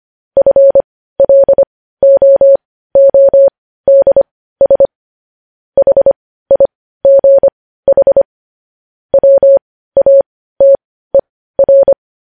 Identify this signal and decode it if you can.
Morse code, heard as a keyed tone, about 25 wpm, spaced as for 15 wpm, 550 Hz. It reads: FLOODS HIGH WATER